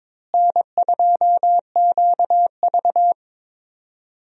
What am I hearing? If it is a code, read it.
Morse code, heard as a keyed tone, about 23 wpm, 700 Hz. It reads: N2QV
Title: USA Call Signs - 4 Characters - Repeated 2x - 22wpm